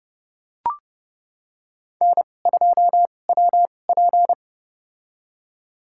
Morse code, heard as a keyed tone, about 31 wpm, spaced as for 22 wpm, 700 Hz. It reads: N2WP
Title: USA Call Signs - 2x Character Spacing - 30wpm